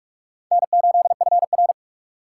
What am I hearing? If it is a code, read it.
Morse code, heard as a keyed tone, about 42 wpm, 700 Hz. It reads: N8FR
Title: USA Call Signs - 4 Characters - 3x Character Spacing - 45wpm